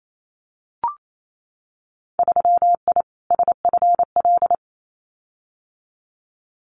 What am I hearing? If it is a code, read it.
Morse code, heard as a keyed tone, about 28 wpm, 700 Hz. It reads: 3I SFL
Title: Winter Field Day Exchanges 28wpm